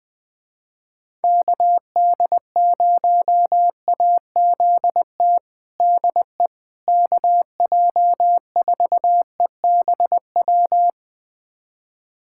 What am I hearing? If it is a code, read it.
Morse code, heard as a keyed tone, about 20 wpm, 700 Hz. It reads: KD0AZT DE KJ4EBW
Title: USA Callsign DE Callsign 20wpm